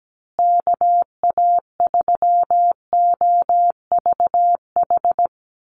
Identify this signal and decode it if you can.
Morse code, heard as a keyed tone, about 17 wpm, 700 Hz. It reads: KA3OVH
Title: USA Call Signs Repeated 2x 17wpm